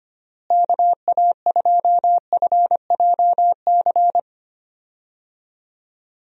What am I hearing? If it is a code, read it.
Morse code, heard as a keyed tone, about 25 wpm, 700 Hz. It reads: KA2FJC
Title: USA Call Signs - 6 Characters - Repeated 2x - 25wpm